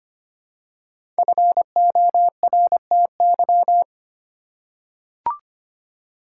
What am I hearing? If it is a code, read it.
Morse code, heard as a keyed tone, about 25 wpm, 700 Hz. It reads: FORTY
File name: 5 Letter Words - First 3 Letters Given - 25wpm